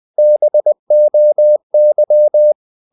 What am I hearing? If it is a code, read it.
Morse code, heard as a keyed tone, about 20 wpm, 600 Hz. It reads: BOY
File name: the old man and the sea / the old man and the sea part 1 20wpm